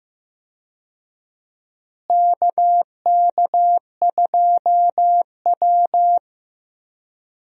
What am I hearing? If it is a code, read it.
Morse code, heard as a keyed tone, about 15 wpm, 700 Hz. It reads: KK2W
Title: USA Call Signs - 4 Characters - Repeated 2x - 15wpm